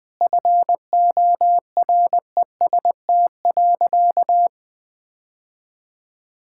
Morse code, heard as a keyed tone, about 20 wpm, 700 Hz. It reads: FOREST.
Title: Sentences with 5-Word Phrases - Spoken Beginning - 20wpm